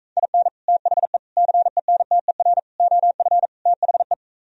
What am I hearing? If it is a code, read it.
Morse code, heard as a keyed tone, about 45 wpm, 700 Hz. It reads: IN THE CENTER OF THE